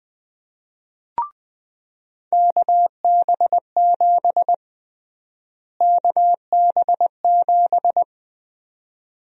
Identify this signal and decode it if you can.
Morse code, heard as a keyed tone, about 20 wpm, 700 Hz. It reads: KB7 KB7
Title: USA Call Sign Prefixes - Repeated 2x - 20wpm